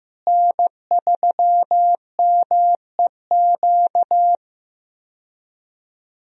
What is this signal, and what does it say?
Morse code, heard as a keyed tone, about 15 wpm, 700 Hz. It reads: N3MEQ
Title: USA Call Signs - 5 Characters - Repeated 2x - 15wpm